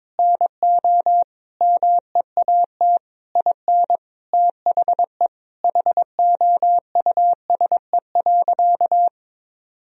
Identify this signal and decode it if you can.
Morse code, heard as a keyed tone, about 22 wpm, 700 Hz. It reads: NO MEAT IN THE HOUSE.
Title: Sentences from Top 1000 Words - Review Part 3 - 22wpm